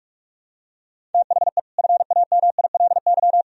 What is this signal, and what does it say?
Morse code, heard as a keyed tone, about 44 wpm, 700 Hz. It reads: THE FAMILY
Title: Sentences from Top 1000 Words - Review Part 2 - 45wpm